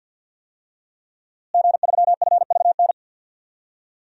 Morse code, heard as a keyed tone, about 53 wpm, 700 Hz. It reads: G3FVN
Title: DXCC-100 Call Signs - 5 Characters - Repeated 3x - 50wpm